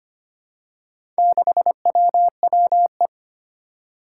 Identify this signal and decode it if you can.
Morse code, heard as a keyed tone, about 25 wpm, 700 Hz. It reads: 6WWE
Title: Four Challenging Letter-Number Combinations 25wpm